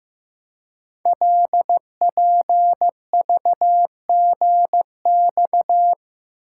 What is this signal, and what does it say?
Morse code, heard as a keyed tone, about 15 wpm, 700 Hz. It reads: LPVGX
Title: Code Groups - 5 Letters - 15wpm